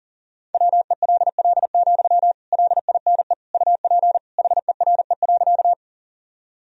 Morse code, heard as a keyed tone, about 40 wpm, 700 Hz. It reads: WELL, LINE UP HERE.
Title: Sentences from Top 1000 Words - Review Part 1 - 40wpm